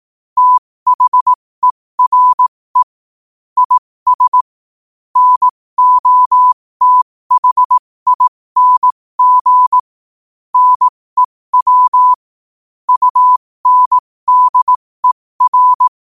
Morse code, heard as a keyed tone, about 18 wpm, spaced as for 15 wpm, 1000 Hz. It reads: THERE IS NOTHING NEW UNDER
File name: qotdfiles / 15wpm